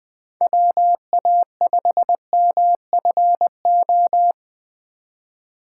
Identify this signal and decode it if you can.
Morse code, heard as a keyed tone, about 20 wpm, 700 Hz. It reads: WA5MFO
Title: Speed Racing - USA 2x3 Call Signs 17wpm